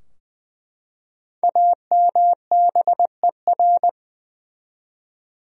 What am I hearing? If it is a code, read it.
Morse code, heard as a keyed tone, about 21 wpm, 700 Hz. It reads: AMBER
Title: USA Common Names - 20wpm